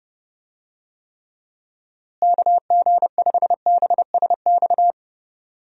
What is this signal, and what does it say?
Morse code, heard as a keyed tone, about 30 wpm, 700 Hz. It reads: KG5BSX
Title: USA Call Signs - 6 Characters - Repeated 3x - 30wpm